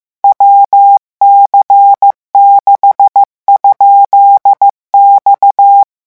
Morse code, heard as a keyed tone, about 15 wpm, 800 Hz. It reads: WC6?X